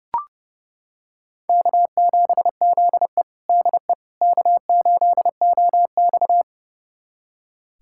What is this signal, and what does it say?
Morse code, heard as a keyed tone, about 30 wpm, 700 Hz. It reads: K7ZE DE K8OX
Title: USA Callsign DE Callsign - 4 Characters - 30wpm